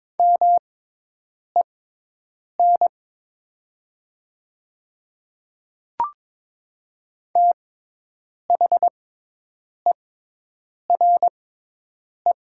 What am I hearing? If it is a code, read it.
Morse code, heard as a keyed tone, about 22 wpm, spaced as for 8 wpm, 700 Hz. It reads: MEN THERE
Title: 6x Character Spacing - Top 200 Words - 22wpm